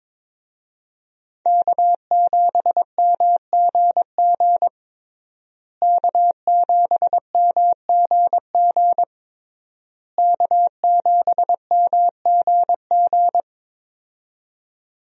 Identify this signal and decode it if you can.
Morse code, heard as a keyed tone, about 21 wpm, 700 Hz. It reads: K7MGG K7MGG K7MGG
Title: USA Call Signs - 5 Characters - Repeated 3x - 22wpm